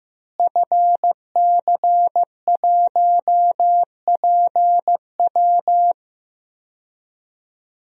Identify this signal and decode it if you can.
Morse code, heard as a keyed tone, about 15 wpm, 700 Hz. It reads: FC1PW
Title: DXCC-100 Call Signs - 5 Characters - Repeated 3x - 15wpm